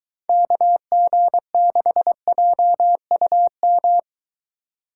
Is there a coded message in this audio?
Yes: Morse code from 0.3 to 4.0 s, heard as a keyed tone, about 22 wpm, 700 Hz. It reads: KG6JUM